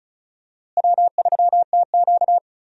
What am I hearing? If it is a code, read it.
Morse code, heard as a keyed tone, about 34 wpm, 700 Hz. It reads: W3TQ
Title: USA Call Signs - 4 Characters - Repeated 2x - 35wpm